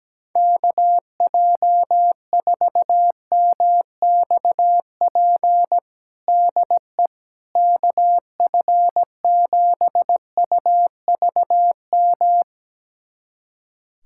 Morse code, heard as a keyed tone, about 17 wpm, 700 Hz. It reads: KJ4MXP DE KF7UVM
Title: USA Callsign DE Callsign 17wpm